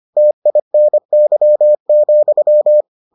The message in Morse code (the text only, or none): TINY,